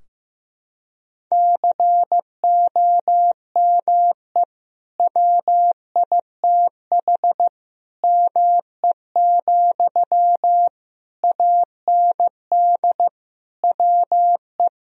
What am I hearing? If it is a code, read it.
Morse code, heard as a keyed tone, about 15 wpm, 700 Hz. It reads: COME WITH ME, AND WE
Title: Sentences from Top 1000 Words - Review Part 2 - 15wpm